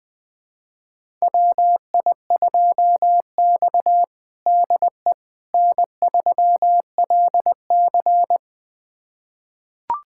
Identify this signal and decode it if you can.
Morse code, heard as a keyed tone, about 21 wpm, 700 Hz. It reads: WI2X DE N3LC
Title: USA Callsign DE Callsign - 4 Characters - 20wpm